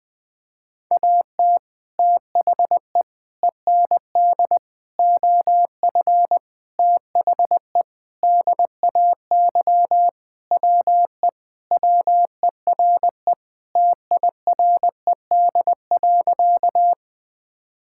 Morse code, heard as a keyed tone, about 20 wpm, 700 Hz. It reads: AT THE END OF THE DAY WE WERE TIRED.
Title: Sentences with 5-Word Phrases - Spoken Beginning - 20wpm